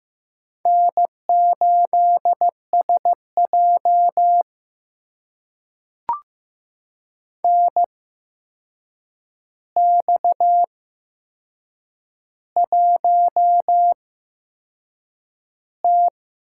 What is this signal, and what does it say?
Morse code, heard as a keyed tone, about 15 wpm, spaced as for 6 wpm, 700 Hz. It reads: N8SJ NX1T
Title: USA Call Signs - 4 Characters - 8x Character Spacing - 15wpm